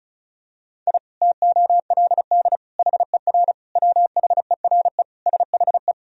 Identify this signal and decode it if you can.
Morse code, heard as a keyed tone, about 36 wpm, 700 Hz. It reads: I TOLD HER WHERE SHE
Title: Sentences from Top 1000 Words - Review Part 2 - 35wpm